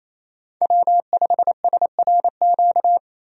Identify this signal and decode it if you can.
Morse code, heard as a keyed tone, about 28 wpm, 700 Hz. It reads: W5SRQ
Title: USA Call Signs - 5 Characters - 28wpm